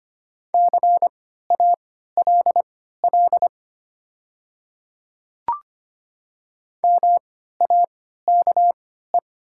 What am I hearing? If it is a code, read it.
Morse code, heard as a keyed tone, about 25 wpm, spaced as for 14 wpm, 700 Hz. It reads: CALL MAKE
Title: 3x Character Spacing - Top 300 Words - 25wpm